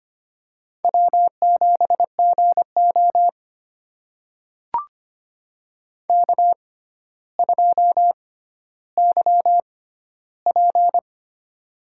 Morse code, heard as a keyed tone, about 25 wpm, spaced as for 13 wpm, 700 Hz. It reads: W7GO K2YP
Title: USA Call Signs - 4 Characters - 6x Character Spacing - 25wpm